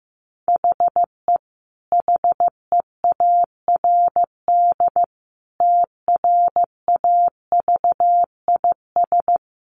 Morse code, heard as a keyed tone, about 15 wpm, 700 Hz. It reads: HE HEARD TRAVIS
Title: A Sound Of Thunder 15wpm